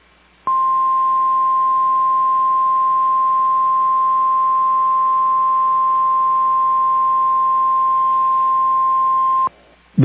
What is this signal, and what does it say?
Emergency Alert System (SAME) tones (the 1050 Hz attention tone). It is heard from 0.4 to 9.5 s.
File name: EAS Logs / KMNR